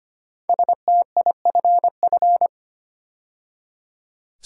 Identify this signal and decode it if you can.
Morse code, heard as a keyed tone, about 25 wpm, 700 Hz. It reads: STIFF
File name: 5 Letter Words - First 3 Letters Given - 25wpm